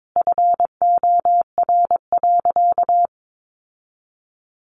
Morse code, heard as a keyed tone, about 22 wpm, 700 Hz. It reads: FOR.
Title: Sentences from Top 800 Words 22wpm